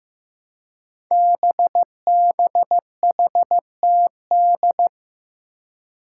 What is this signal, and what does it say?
Morse code, heard as a keyed tone, about 15 wpm, 700 Hz. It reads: BBHTD
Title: Five Challenging Letter-Number Combinations 15wpm